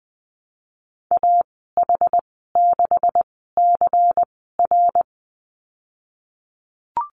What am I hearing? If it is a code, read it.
Morse code, heard as a keyed tone, about 20 wpm, spaced as for 14 wpm, 700 Hz. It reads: AH6CR
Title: 186 Lesson 30g - Letter b - Call Signs 20wpm